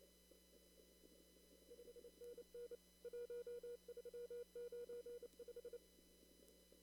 Morse code, heard as a keyed tone, about 28 wpm, 480 Hz. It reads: 5NN 1395
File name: CW / kt5j wpx cw-op-k5pi